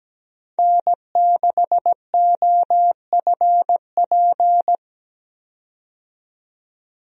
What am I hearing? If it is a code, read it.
Morse code, heard as a keyed tone, about 17 wpm, 700 Hz. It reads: N6OFP
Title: USA Call Signs - 5 Characters - 17wpm